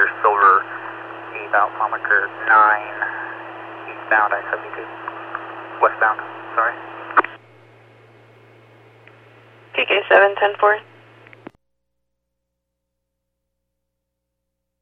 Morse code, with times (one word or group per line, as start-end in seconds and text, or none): none